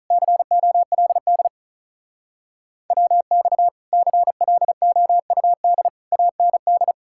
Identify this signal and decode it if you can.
Morse code, heard as a keyed tone, about 48 wpm, 700 Hz. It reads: COLD WX CLOUD AND